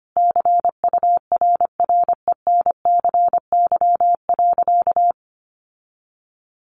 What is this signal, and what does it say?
Morse code, heard as a keyed tone, about 25 wpm, 700 Hz. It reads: CURRENCY.
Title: Sentences from Top 1500 Words 25wpm